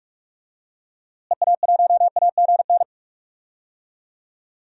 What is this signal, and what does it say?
Morse code, heard as a keyed tone, about 42 wpm, 700 Hz. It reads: EA1AGN
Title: DXCC-100 Call Signs - 6 Characters - Repeated 3x - 45wpm